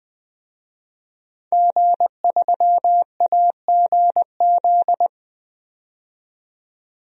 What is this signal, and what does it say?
Morse code, heard as a keyed tone, about 20 wpm, 700 Hz. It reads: G3AGZ
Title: DXCC-100 Call Signs - 5 Characters - 20wpm